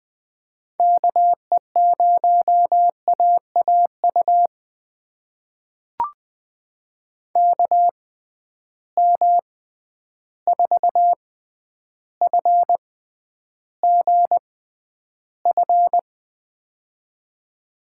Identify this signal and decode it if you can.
Morse code, heard as a keyed tone, about 20 wpm, spaced as for 10 wpm, 700 Hz. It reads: KE0AAU KM4FGF
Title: USA Call Signs - 6 Characters - 6x Character Spacing - 20wpm